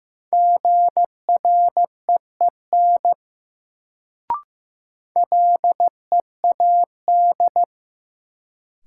Morse code, heard as a keyed tone, about 15 wpm, 700 Hz. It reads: GREEN LEAD
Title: Sentences from Top 500 Words - New Word Drill - 15wpm